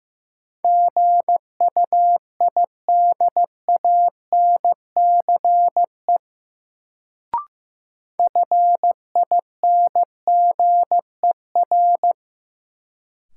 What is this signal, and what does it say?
Morse code, heard as a keyed tone, about 15 wpm, 700 Hz. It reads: GUIDANCE FINGER
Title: Sentences from Top 1600 Words - New Word Drill - 15wpm